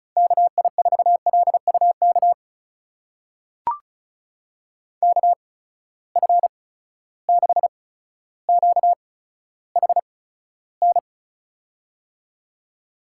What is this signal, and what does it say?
Morse code, heard as a keyed tone, about 34 wpm, spaced as for 15 wpm, 700 Hz. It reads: KI4LUK KF6QHN
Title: USA Call Signs - 8x Character Spacing - 35wpm